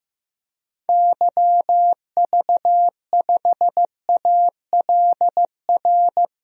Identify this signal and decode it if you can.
Morse code, heard as a keyed tone, about 16 wpm, 700 Hz. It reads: YV5ALR
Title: DXCC-100 Call Signs - 6 Characters - Repeated 2x - 15wpm